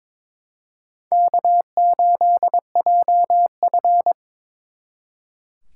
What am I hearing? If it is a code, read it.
Morse code, heard as a keyed tone, about 26 wpm, 700 Hz. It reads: K8JF
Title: Speed Racing - USA 1x2 Call Signs 22wpm